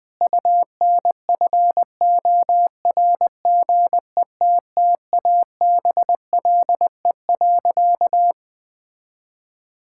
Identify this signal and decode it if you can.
Morse code, heard as a keyed tone, about 20 wpm, 700 Hz. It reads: UNFORGETTABLE.
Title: Sentences with 6-Word Phrases - Spoken Beginning - 20wpm